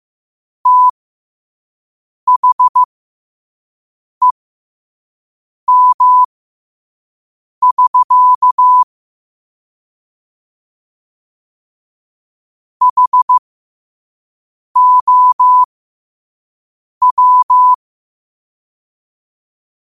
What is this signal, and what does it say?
Morse code, heard as a keyed tone, about 15 wpm, spaced as for 5 wpm, 1000 Hz. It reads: THEM<SK> HOW
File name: qotdfiles / 5wpm